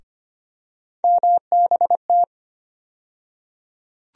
Morse code, heard as a keyed tone, about 25 wpm, 700 Hz. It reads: MBT
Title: USA Call Sign Postfixes - 3 Letters 25wpm